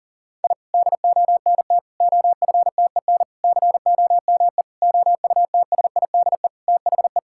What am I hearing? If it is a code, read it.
Morse code, heard as a keyed tone, about 40 wpm, 700 Hz. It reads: I DONT OFTEN COME OUTSIDE THE
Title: Sentences from Top 1000 Words - Review Part 6 - 40wpm